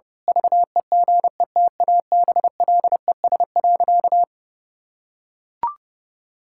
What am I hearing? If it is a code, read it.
Morse code, heard as a keyed tone, about 30 wpm, 700 Hz. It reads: VEGETABLES.
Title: Pangram Challenge v2 - 30wpm